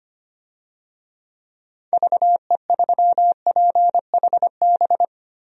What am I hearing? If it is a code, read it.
Morse code, heard as a keyed tone, about 25 wpm, 700 Hz. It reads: VE3PHB
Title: DXCC-100 Call Signs - 6 Characters - Repeated 3x - 25wpm